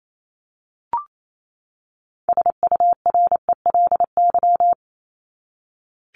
Morse code, heard as a keyed tone, about 28 wpm, 700 Hz. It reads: SURELY
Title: LY Suffix - Top 100 Words with Base Words 28wpm